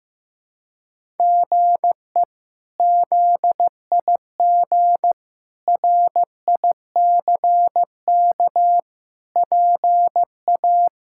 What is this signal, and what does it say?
Morse code, heard as a keyed tone, about 15 wpm, 700 Hz. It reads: GE ZIG RICK PA
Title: K1USN SST QSO Practice - 15wpm